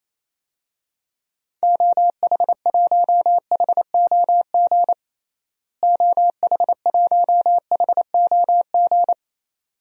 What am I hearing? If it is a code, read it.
Morse code, heard as a keyed tone, about 28 wpm, 700 Hz. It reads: OH1HOG OH1HOG
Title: DXCC-100 Call Signs - 6 Characters - Repeated 2x - 28wpm